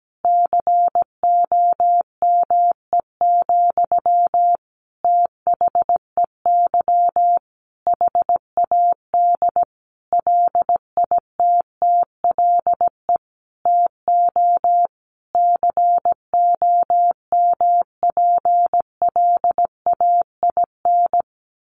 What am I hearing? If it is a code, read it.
Morse code, heard as a keyed tone, about 17 wpm, 700 Hz. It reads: COME, THEY HAD LITTLE TO COMPLAIN
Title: Sentences from Top 1300 Words 17wpm